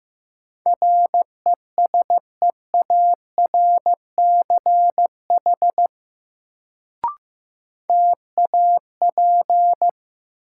Morse code, heard as a keyed tone, about 15 wpm, 700 Hz. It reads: RESEARCH TAP
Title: Sentences from Top 1400 Words - New Word Drill Alt - 15wpm